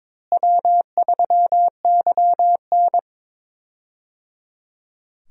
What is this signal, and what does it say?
Morse code, heard as a keyed tone, about 21 wpm, 700 Hz. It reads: W3YN
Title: USA Call Signs - 4 Characters - Repeated 3x - 22wpm